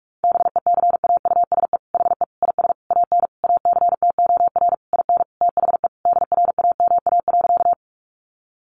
Morse code, heard as a keyed tone, about 44 wpm, 700 Hz. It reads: BECAUSE HE IS AN ACTOR IN THE DRAMA.
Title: Sentences from Top 1500 Words 45wpm